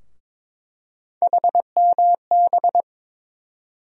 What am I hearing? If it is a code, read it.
Morse code, heard as a keyed tone, about 22 wpm, 700 Hz. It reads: HMB
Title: USA Call Sign Postfixes - 3 Letters 22wpm